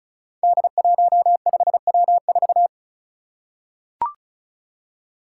Morse code, heard as a keyed tone, about 36 wpm, 700 Hz. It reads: D15W4
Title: Five Challenging Letter-Number Combinations 35wpm